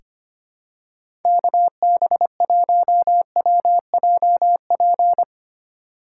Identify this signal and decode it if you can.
Morse code, heard as a keyed tone, about 25 wpm, 700 Hz. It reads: KB1WJP
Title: USA Call Signs - 2x Character Spacing - 25wpm